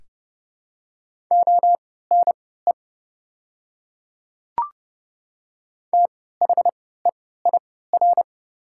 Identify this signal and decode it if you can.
Morse code, heard as a keyed tone, about 30 wpm, spaced as for 17 wpm, 700 Hz. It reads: ONE THEIR
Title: 3x Character Spacing - Top 100 Words - 30wpm